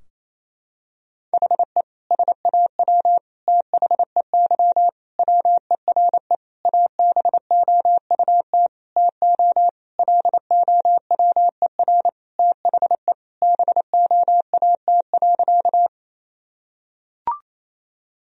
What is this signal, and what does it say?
Morse code, heard as a keyed tone, about 28 wpm, 700 Hz. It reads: HE SAW THEY WERE ABOUT TO LOWER THE BOAT.
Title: Sentences from Top 1000 Words - Review Part 4 - 28wpm